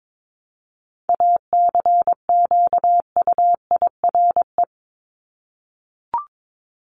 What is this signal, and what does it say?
Morse code, heard as a keyed tone, about 22 wpm, 700 Hz. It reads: ACQUIRE
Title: Top 1501-1750 Words 22wpm